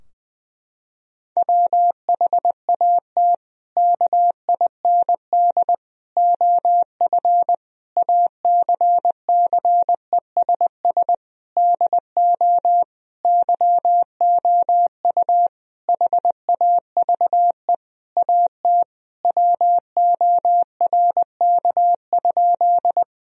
Morse code, heard as a keyed tone, about 21 wpm, 700 Hz. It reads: WHAT KIND OF ACCESS DO YOU HAVE AT WORK?
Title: Sentences from Top 1000 Words - Review Part 4 - 20wpm